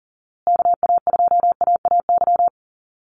Morse code, heard as a keyed tone, about 40 wpm, 700 Hz. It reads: KA2AAY
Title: USA Call Signs Repeated 3x 40wpm